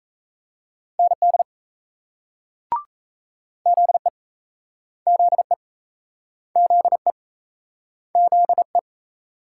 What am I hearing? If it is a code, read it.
Morse code, heard as a keyed tone, about 38 wpm, 700 Hz. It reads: ND ZE ZE ZE ZE